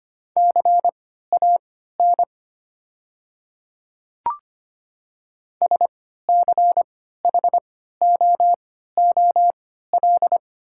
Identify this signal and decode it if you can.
Morse code, heard as a keyed tone, about 25 wpm, spaced as for 14 wpm, 690 Hz. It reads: CAN SCHOOL
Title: 3x Character Spacing - Top 300 Words - 25wpm